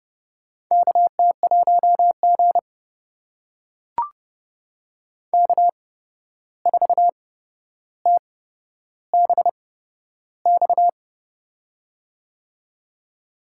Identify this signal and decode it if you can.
Morse code, heard as a keyed tone, about 30 wpm, spaced as for 12 wpm, 700 Hz. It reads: KT1G K4TBX
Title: USA Call Signs - 8x Character Spacing - 30wpm